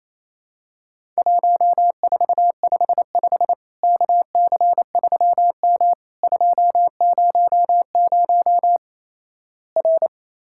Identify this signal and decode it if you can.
Morse code, heard as a keyed tone, about 28 wpm, 700 Hz. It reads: 1455 KC3M 200 R
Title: Worked All Europe DX Contest - QTC Exchange - 28wpm